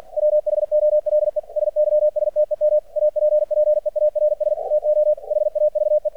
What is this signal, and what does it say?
Morse code, heard as a keyed tone, about 50 wpm, 600 Hz. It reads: WHOLE SYSTEM APPEARS TO HAVE
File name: archived / CRSnet